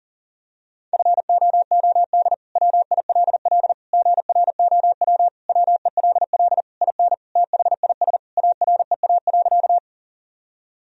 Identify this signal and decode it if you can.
Morse code, heard as a keyed tone, about 40 wpm, 700 Hz. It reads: FOOD WILL GROW WELL IN THIS AREA.
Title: Sentences from Top 1000 Words - Review Part 2 - 40wpm